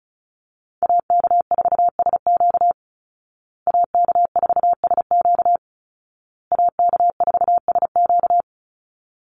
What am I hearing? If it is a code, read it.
Morse code, heard as a keyed tone, about 34 wpm, 700 Hz. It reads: AK4SQ AK4SQ AK4SQ
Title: Call Signs Repeated 2x - 35wpm